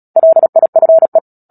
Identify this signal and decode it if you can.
Morse code, heard as a keyed tone, about 37 wpm, 650 Hz. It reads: LIFE
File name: Book Two / 10-Chapter 10 - Epilogue - 36 WPM